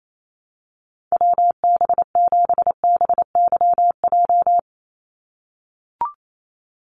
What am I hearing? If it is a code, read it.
Morse code, heard as a keyed tone, about 28 wpm, 700 Hz. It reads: WB7BYJ
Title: USA Call Signs 28wpm